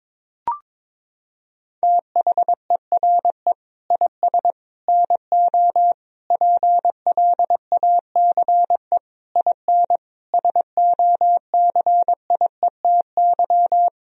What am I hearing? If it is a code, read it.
Morse code, heard as a keyed tone, about 22 wpm, 700 Hz. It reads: THERE IS NO PLACE IN SOCIETY
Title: Sentences from Top 1000 Words - Review Part 4 - 22wpm